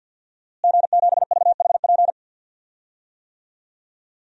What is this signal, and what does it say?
Morse code, heard as a keyed tone, about 51 wpm, 700 Hz. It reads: G7VHP
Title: DXCC-100 Call Signs - 5 Characters - Repeated 2x - 50wpm